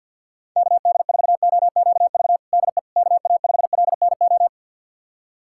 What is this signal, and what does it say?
Morse code, heard as a keyed tone, about 50 wpm, 700 Hz. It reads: KD4OQV DE KA5LNO
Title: USA Callsign DE Callsign - 6 Characters - 50wpm